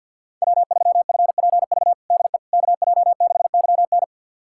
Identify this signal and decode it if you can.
Morse code, heard as a keyed tone, about 50 wpm, 690 Hz. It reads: W3FPV DE KJ6YN
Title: USA Callsign DE Callsign - 5 Characters - 50wpm